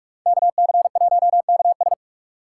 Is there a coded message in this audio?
Yes: Morse code from 0.3 to 1.9 s, heard as a keyed tone, about 44 wpm, 700 Hz. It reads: KC1KS